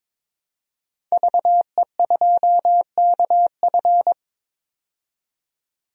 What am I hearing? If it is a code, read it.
Morse code, heard as a keyed tone, about 23 wpm, 700 Hz. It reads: VE2KF
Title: DXCC-100 Call Signs - 5 Characters - 22wpm